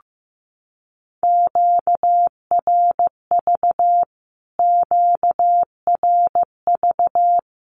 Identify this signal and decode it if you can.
Morse code, heard as a keyed tone, about 15 wpm, 700 Hz. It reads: QRV QRV
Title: Q-Codes Repeated 3x 15wpm